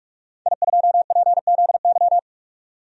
Morse code, heard as a keyed tone, about 46 wpm, 700 Hz. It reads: I2PZY
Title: DXCC-100 Call Signs - 5 Characters - 45wpm